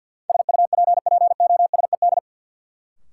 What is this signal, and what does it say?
Morse code, heard as a keyed tone, about 49 wpm, 700 Hz. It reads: SUPPOSED
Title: Sets of 3 Words - New Word Drill - 50wpm